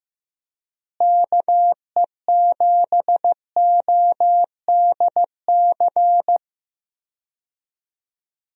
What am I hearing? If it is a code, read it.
Morse code, heard as a keyed tone, about 15 wpm, 700 Hz. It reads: KE7ODC
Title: USA Call Signs - 6 Characters - Repeated 3x - 15wpm